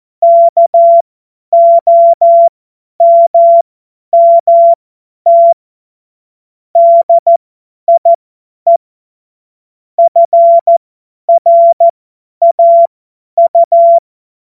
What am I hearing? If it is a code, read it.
Morse code, heard as a keyed tone, about 14 wpm, spaced as for 10 wpm, 680 Hz. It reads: KOMMT DIE FRAU
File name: Morsekurs / Witze / Witz 11. Januar 2022, 50 F70